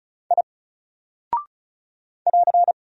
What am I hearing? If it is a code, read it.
Morse code, heard as a keyed tone, about 36 wpm, 700 Hz. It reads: I <AR>
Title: Single Character (Letter:Number:Punctuation:Prosign) Alt 35wpm